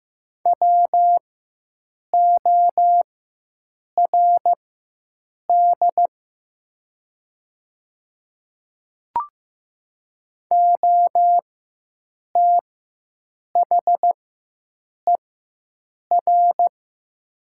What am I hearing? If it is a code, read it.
Morse code, heard as a keyed tone, about 15 wpm, spaced as for 7 wpm, 700 Hz. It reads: WORD OTHER
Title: 4x Character Spacing - Top 100 Words - 15wpm